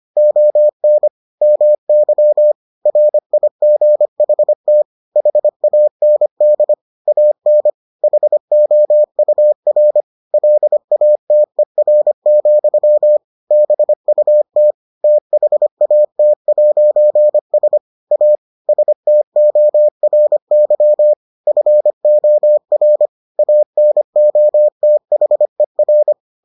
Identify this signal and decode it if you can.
Morse code, heard as a keyed tone, about 25 wpm, 600 Hz. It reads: ON MY RIGHT HAND AN HOUR LATER, BUT THAT'S A STORY FOR ANOTHER